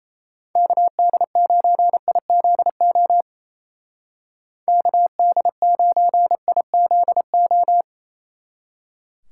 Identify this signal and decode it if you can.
Morse code, heard as a keyed tone, about 35 wpm, 700 Hz. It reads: KD9IZO KD9IZO
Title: Speed Racing - USA 2x3 Call Signs 28wpm